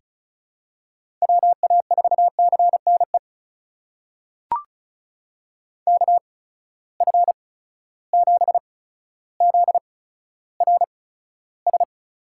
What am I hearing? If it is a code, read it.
Morse code, heard as a keyed tone, about 34 wpm, spaced as for 15 wpm, 700 Hz. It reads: WA4CNE KF7ZRS
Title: USA Call Signs - 6 Characters - 8x Character Spacing - 35wpm